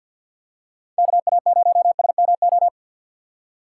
Morse code, heard as a keyed tone, about 52 wpm, 700 Hz. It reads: KA0SMO
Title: USA Call Signs - 6 Characters - 6x Character Spacing - 50wpm